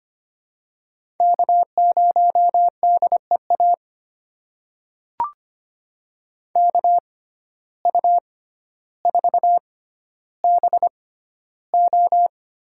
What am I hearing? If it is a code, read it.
Morse code, heard as a keyed tone, about 25 wpm, spaced as for 13 wpm, 700 Hz. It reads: K0DEA KU4BO
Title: USA Call Signs - 5 Characters - 6x Character Spacing - 25wpm